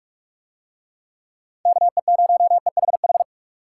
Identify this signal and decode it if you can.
Morse code, heard as a keyed tone, about 46 wpm, 700 Hz. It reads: KE0EHH